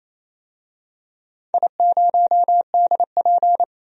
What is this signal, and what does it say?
Morse code, heard as a keyed tone, about 28 wpm, 700 Hz. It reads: I0DP
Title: DXCC-100 Call Signs - 4 Characters - 28wpm